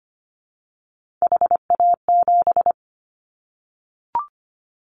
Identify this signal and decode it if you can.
Morse code, heard as a keyed tone, about 25 wpm, 700 Hz. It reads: HA7
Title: Callsign Prefixes for Top 25 DXCC - 25wpm